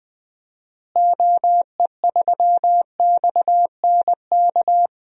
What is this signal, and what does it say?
Morse code, heard as a keyed tone, about 20 wpm, 700 Hz. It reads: OE3XNK
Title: DXCC-100 Call Signs - 6 Characters - Repeated 3x - 20wpm